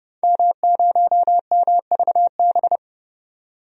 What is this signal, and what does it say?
Morse code, heard as a keyed tone, about 31 wpm, 700 Hz. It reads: M0MVB
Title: DXCC-100 Call Signs - 5 Characters - 30wpm